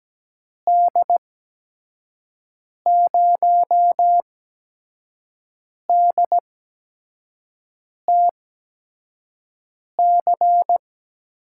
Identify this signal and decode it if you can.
Morse code, heard as a keyed tone, about 17 wpm, spaced as for 5 wpm, 700 Hz. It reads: D0DTC
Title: USA Call Signs - 6 Characters - 8x Character Spacing - 17wpm